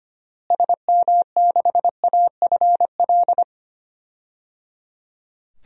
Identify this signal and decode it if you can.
Morse code, heard as a keyed tone, about 25 wpm, 700 Hz. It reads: SM6AFL
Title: DXCC-100 Call Signs - 6 Characters - Repeated 3x - 25wpm